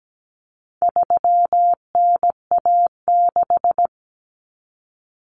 Morse code, heard as a keyed tone, about 17 wpm, 700 Hz. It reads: 3NA6
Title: Common Cut Numbers - 4-Digit 17wpm